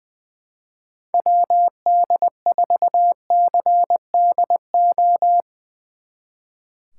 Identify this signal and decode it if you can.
Morse code, heard as a keyed tone, about 20 wpm, 700 Hz. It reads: WD4CDO
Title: Speed Racing - USA 2x3 Call Signs 20wpm